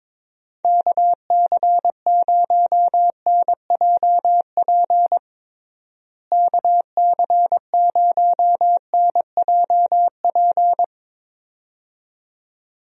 Morse code, heard as a keyed tone, about 22 wpm, 700 Hz. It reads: KC0NJP KC0NJP
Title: USA Call Signs - 6 Characters - Repeated 2x - 22wpm